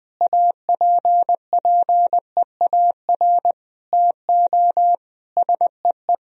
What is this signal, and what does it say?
Morse code, heard as a keyed tone, about 21 wpm, 700 Hz. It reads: APPEAR TO SEE
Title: Sentences from Top 1000 Words - Review Part 2 - 20wpm